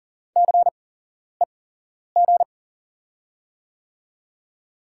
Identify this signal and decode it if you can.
Morse code, heard as a keyed tone, about 43 wpm, spaced as for 11 wpm, 700 Hz. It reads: CEG
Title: USA Call Signs - 6 Characters - 8x Character Spacing - 40wpm